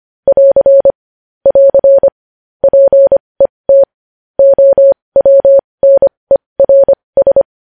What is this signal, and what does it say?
Morse code, heard as a keyed tone, about 25 wpm, spaced as for 20 wpm, 550 Hz. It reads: <AR> <AR> PET OWNERS
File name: podcast / News headlines-1x-25x20